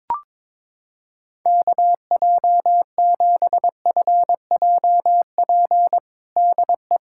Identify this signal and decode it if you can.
Morse code, heard as a keyed tone, about 21 wpm, 700 Hz. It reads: KJ7FJP DE
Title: USA Callsign DE Callsign - 6 Characters - 22wpm